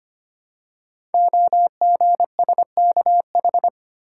Morse code, heard as a keyed tone, about 25 wpm, 700 Hz. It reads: OGSKH
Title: Code Groups - 5 Letters - 25wpm